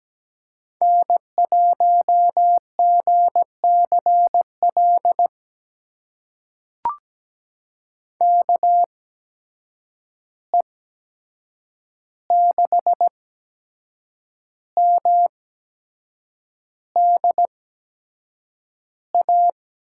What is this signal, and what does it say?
Morse code, heard as a keyed tone, about 17 wpm, spaced as for 7 wpm, 700 Hz. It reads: N1GCL KE6MDA
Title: USA Call Signs - 8x Character Spacing - 17wpm